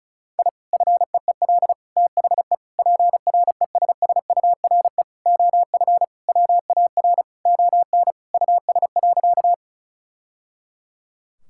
Morse code, heard as a keyed tone, about 36 wpm, 700 Hz. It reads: I FEEL THE PRESSURE OF WAR ON US.
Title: Sentences from Top 1000 Words - Review Part 6 - 35wpm